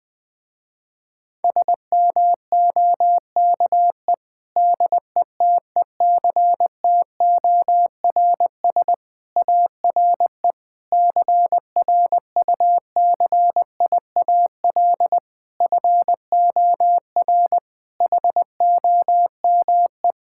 Morse code, heard as a keyed tone, about 20 wpm, 700 Hz. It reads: SMOKE DETECTORS ARE CRUCIAL FOR HOME
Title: Difficult Sentences with Hints - 20wpm